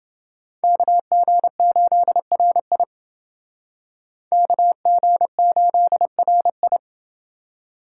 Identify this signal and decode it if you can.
Morse code, heard as a keyed tone, about 29 wpm, 700 Hz. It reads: KG8RI KG8RI